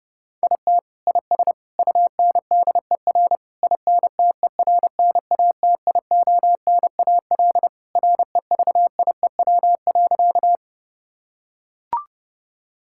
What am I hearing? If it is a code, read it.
Morse code, heard as a keyed tone, about 32 wpm, 700 Hz. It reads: IT IS UNDER INTERNATIONAL REVIEW.
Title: Sentences from Top 1000 Words - Review Part 6 - 30wpm